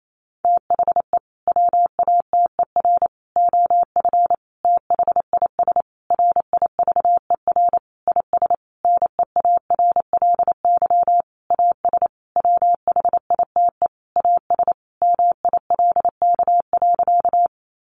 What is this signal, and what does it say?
Morse code, heard as a keyed tone, about 28 wpm, 700 Hz. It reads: THE WATER OF THIS RIVER IS NEARLY AS WHITE AS MILK.
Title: Sentences from Top 1500 Words 28wpm